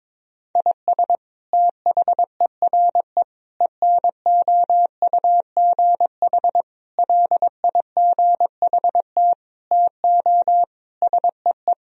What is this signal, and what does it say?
Morse code, heard as a keyed tone, about 22 wpm, 700 Hz. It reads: IS THERE ENOUGH LIGHT TO SEE
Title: Sentences from Top 1000 Words - Review Part 1 - 22wpm